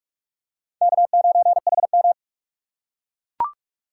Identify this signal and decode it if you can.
Morse code, heard as a keyed tone, about 45 wpm, 700 Hz. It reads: K9HM
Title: USA Call Signs - 4 Characters - 2x Character Spacing - 45wpm